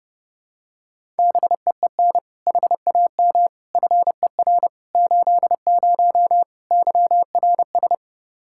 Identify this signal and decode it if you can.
Morse code, heard as a keyed tone, about 30 wpm, 700 Hz. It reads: BEEN HAM FER 80 YRS
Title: K1USN SST QSO Practice 30wpm